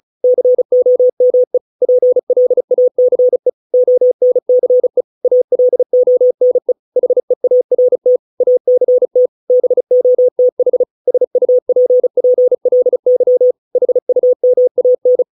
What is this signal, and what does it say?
Morse code, heard as a keyed tone, about 34 wpm, 500 Hz. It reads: COME PLACE ONCE ALONE HEART ACT BOTH SUPPLY HUMAN